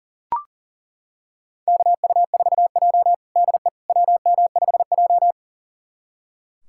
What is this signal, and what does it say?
Morse code, heard as a keyed tone, about 40 wpm, 700 Hz. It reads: KU4J DE WM5J
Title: USA Callsign DE Callsign - 4 Characters - 40wpm